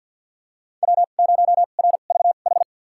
Morse code, heard as a keyed tone, about 46 wpm, spaced as for 35 wpm, 700 Hz. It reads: W0RVH
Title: USA Call Signs - 2x Character Spacing - 50wpm